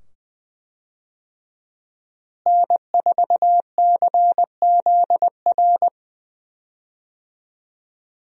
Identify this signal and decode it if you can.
Morse code, heard as a keyed tone, about 20 wpm, 700 Hz. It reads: N4CZR
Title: USA Call Signs - 5 Characters - Repeated 2x - 20wpm